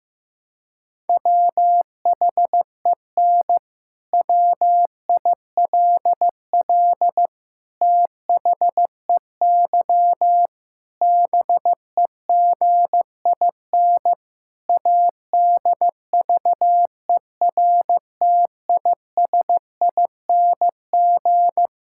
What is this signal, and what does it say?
Morse code, heard as a keyed tone, about 15 wpm, 700 Hz. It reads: WHEN WILL THEY BEGIN ADVERTISING
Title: Sentences from Top 1000 Words - Review Part 6 - 15wpm